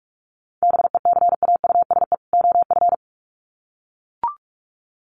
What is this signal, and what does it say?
Morse code, heard as a keyed tone, about 44 wpm, 700 Hz. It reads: BECAUSE OF
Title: Sets of 2 Words 45wpm